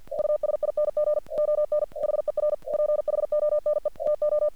Morse code, heard as a keyed tone, about 48 wpm, 600 Hz. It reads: USING ON HER PHONE TO
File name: archived / CRSnet